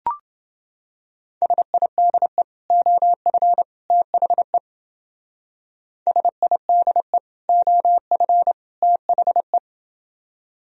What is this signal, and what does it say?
Morse code, heard as a keyed tone, about 29 wpm, 700 Hz. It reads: SIDE OF THE SIDE OF THE